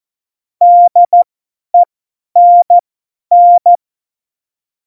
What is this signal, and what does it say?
Morse code, heard as a keyed tone, about 14 wpm, spaced as for 10 wpm, 700 Hz. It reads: DENN
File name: Morsekurs / Witze / Witz 29. September 2020, 50 F70